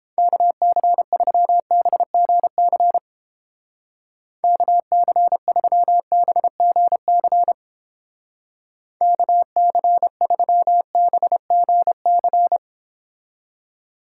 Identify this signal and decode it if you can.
Morse code, heard as a keyed tone, about 31 wpm, 700 Hz. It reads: KC3BGC KC3BGC KC3BGC